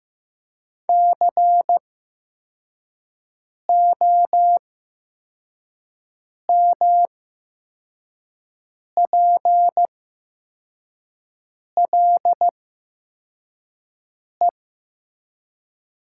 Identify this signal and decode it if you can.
Morse code, heard as a keyed tone, about 15 wpm, spaced as for 4 wpm, 700 Hz. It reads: COMPLE
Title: 8x Character Spacing - Top 500 Words - 15wpm